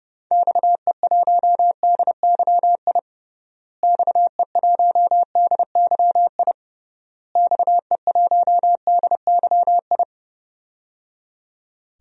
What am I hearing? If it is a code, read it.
Morse code, heard as a keyed tone, about 30 wpm, 700 Hz. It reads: XE1DYI XE1DYI XE1DYI
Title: DXCC-100 Call Signs - 6 Characters - Repeated 3x - 30wpm